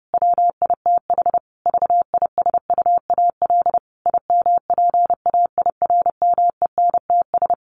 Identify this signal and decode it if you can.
Morse code, heard as a keyed tone, about 28 wpm, 700 Hz. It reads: WITH VISUAL IMPAIRMENTS
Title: Sets of 5 Words Encore 30wpm